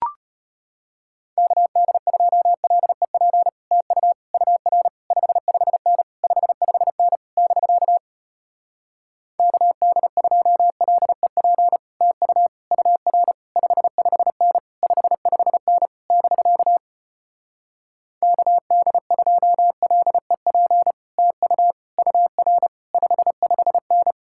Morse code, heard as a keyed tone, about 33 wpm, 700 Hz. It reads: KD2LEP TU UR 55N 55N <BK> KD2LEP TU UR 55N 55N <BK> KD2LEP TU UR 55N